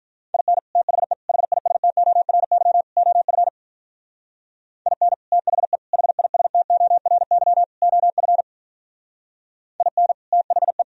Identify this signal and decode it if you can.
Morse code, heard as a keyed tone, about 45 wpm, 700 Hz. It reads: IN THE HISTORY OF IN THE HISTORY OF IN THE